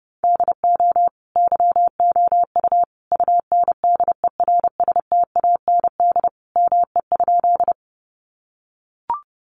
Morse code, heard as a keyed tone, about 30 wpm, 700 Hz. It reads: DO YOU UNDERSTAND ME?
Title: Sentences from Top 600 Words 30wpm